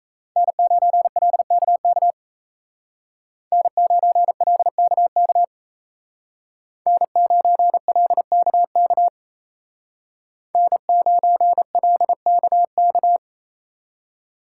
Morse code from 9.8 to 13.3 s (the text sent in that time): N9LKK